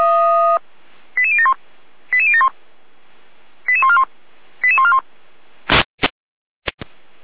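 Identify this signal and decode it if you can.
Five-tone selective call (ZVEI), heard as quick runs of five tones, tones of about 70 ms. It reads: 88071 88071 88141 88141
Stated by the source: ZVEI five-tone selective call received by SDR, archived by Feuerwehr Orlamünde.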